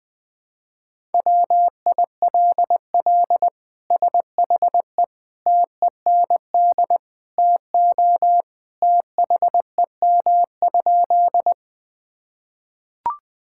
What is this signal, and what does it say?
Morse code, heard as a keyed tone, about 20 wpm, 700 Hz. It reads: WILL SHE TEND TO THEM?
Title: Sentences from Top 1000 Words - Review Part 5 - 20wpm